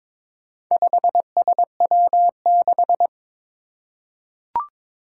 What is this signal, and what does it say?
Morse code, heard as a keyed tone, about 22 wpm, 700 Hz. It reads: 5SW6
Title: Four Challenging Letter-Number Combinations 22wpm